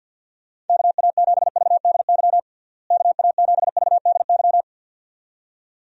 Morse code, heard as a keyed tone, about 50 wpm, 700 Hz. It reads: KA7VDY KA7VDY
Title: USA Call Signs - 6 Characters - Repeated 2x - 50wpm